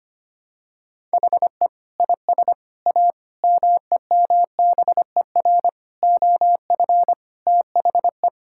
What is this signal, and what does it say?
Morse code, heard as a keyed tone, about 25 wpm, 700 Hz. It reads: HE IS A MEMBER OF THE
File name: practice / 25wpm / Sentences with 3-Word Phrases - 25wpm